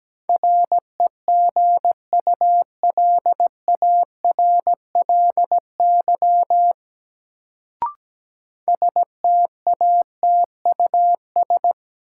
Morse code, heard as a keyed tone, about 17 wpm, 700 Hz. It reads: REGULARLY STATUS
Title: Sentences from Top 1400 Words - New Word Drill - 17wpm